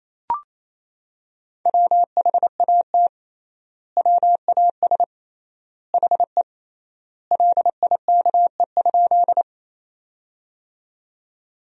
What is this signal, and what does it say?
Morse code, heard as a keyed tone, about 28 wpm, 700 Hz. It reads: WHAT WAS HE LIKE?
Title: 3x Word-Spacing - Sentences from Top 100 Words - 28wpm